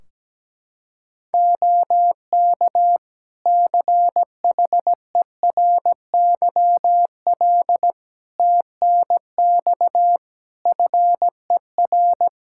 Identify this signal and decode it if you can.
Morse code, heard as a keyed tone, about 17 wpm, 700 Hz. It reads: OK CHERYL TNX FER
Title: Practice QSO - Exchange 4 - Ending - 17wpm